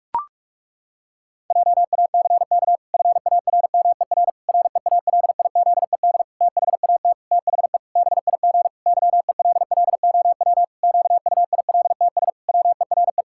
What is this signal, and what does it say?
Morse code, heard as a keyed tone, about 46 wpm, 700 Hz. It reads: JACK FARMER REALIZED THAT THE BIG YELLOW QUILTS WERE
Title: Pangram Challenge v2 - 45wpm